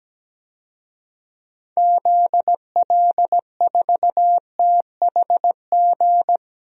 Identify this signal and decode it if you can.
Morse code, heard as a keyed tone, about 17 wpm, 700 Hz. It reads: ZL4THG
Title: DXCC-100 Call Signs - 6 Characters - 17wpm